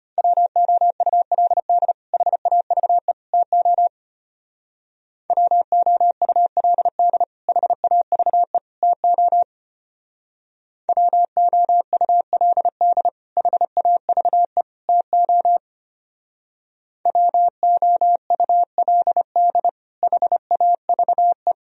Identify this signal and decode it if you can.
Morse code, heard as a keyed tone, about 31 wpm, 700 Hz. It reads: WOULD HAVE TO WOULD HAVE TO WOULD HAVE TO WOULD HAVE
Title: Speed-Racing - Sets of 3 Words 25wpm